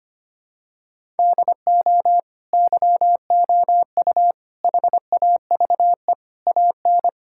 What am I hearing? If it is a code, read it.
Morse code, heard as a keyed tone, about 25 wpm, 700 Hz. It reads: DO YOU HAVE AN
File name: Sets of 9 Words - 25wpm